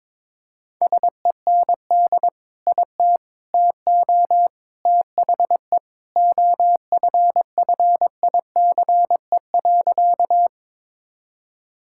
Morse code, heard as a keyed tone, about 22 wpm, 700 Hz. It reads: SEND IT TO THE OFFICE.
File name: practice / 22wpm / Sentences with 3-Word Phrases - 22wpm